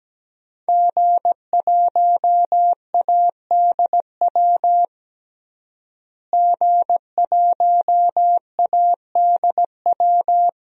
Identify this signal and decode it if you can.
Morse code, heard as a keyed tone, about 17 wpm, 700 Hz. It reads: G1ADW G1ADW
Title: DXCC-100 Call Signs - 5 Characters - Repeated 2x - 17wpm